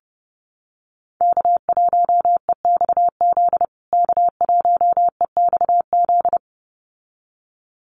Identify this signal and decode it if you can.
Morse code, heard as a keyed tone, about 30 wpm, 700 Hz. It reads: K1EXZ K1EXZ
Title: USA Call Signs Repeated 2x 30wpm